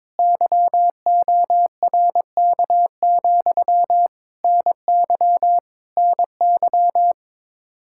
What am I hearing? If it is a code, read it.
Morse code, heard as a keyed tone, about 22 wpm, 700 Hz. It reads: YORK, NY NY
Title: Speed-Racing Common QSO Phrases - 22wpm